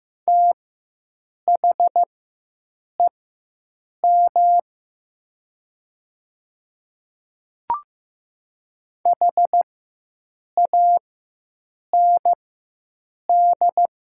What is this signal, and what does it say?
Morse code, heard as a keyed tone, about 15 wpm, spaced as for 7 wpm, 700 Hz. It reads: THEM HAND
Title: 4x Character Spacing - Top 500 Words - 15wpm